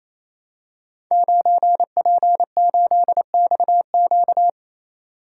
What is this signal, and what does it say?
Morse code, heard as a keyed tone, about 27 wpm, 700 Hz. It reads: 9P8XQ
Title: Code Groups - 5 Letter:Numbers - 28wpm